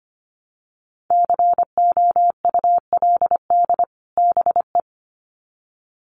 Morse code, heard as a keyed tone, about 25 wpm, 700 Hz. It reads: COULD BE
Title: Sets of 2 Words 25wpm